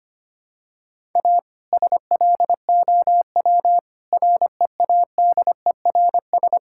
Morse code, heard as a keyed tone, about 25 wpm, 700 Hz. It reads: A SLOW READERS
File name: practice / 25wpm / All-Age Appropriate Jokes - 25wpm